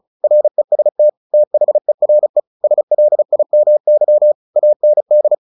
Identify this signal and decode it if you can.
Morse code, heard as a keyed tone, about 34 wpm, 600 Hz. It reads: REST THERE SLIMY AND
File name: the old man and the sea / the old man and the sea 1 35wpm